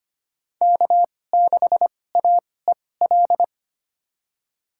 Morse code, heard as a keyed tone, about 25 wpm, spaced as for 18 wpm, 700 Hz. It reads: K6AEL
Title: USA Call Signs - 5 Characters - 2x Character Spacing - 25wpm